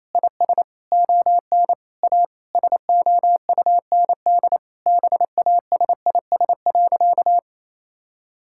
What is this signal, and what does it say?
Morse code, heard as a keyed tone, about 28 wpm, 700 Hz. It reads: IS ON A SOUND BASIS.
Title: Sentences from Top 1000 Words - Review Part 5 - 28wpm